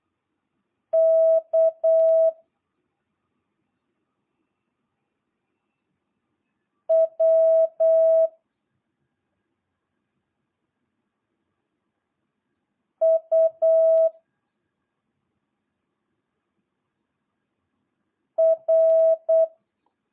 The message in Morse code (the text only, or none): K W U R